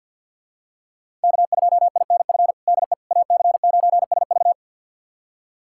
Morse code, heard as a keyed tone, about 52 wpm, 700 Hz. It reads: K2INF DE AC9SV
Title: USA Callsign DE Callsign - 5 Characters - 50wpm